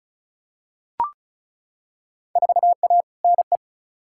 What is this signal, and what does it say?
Morse code, heard as a keyed tone, about 36 wpm, 700 Hz. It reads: 4A NE
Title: Field Day Exchanges - 35wpm